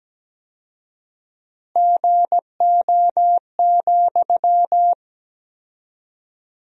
Morse code, heard as a keyed tone, about 17 wpm, 700 Hz. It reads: GO,
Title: 4x Word-Spacing - Sentences from Top 400 Words - 17wpm